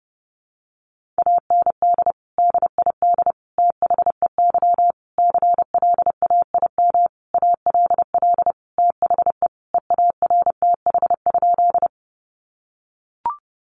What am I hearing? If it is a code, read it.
Morse code, heard as a keyed tone, about 30 wpm, 700 Hz. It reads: AND DID THEY CLAIM ALL THE EARTH?
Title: Sentences from Top 1500 Words 30wpm